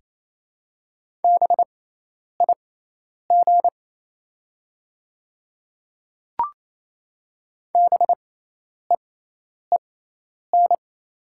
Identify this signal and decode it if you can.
Morse code, heard as a keyed tone, about 28 wpm, spaced as for 10 wpm, 700 Hz. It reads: BIG BEEN
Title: 6x Character Spacing - Top 500 Words - 28wpm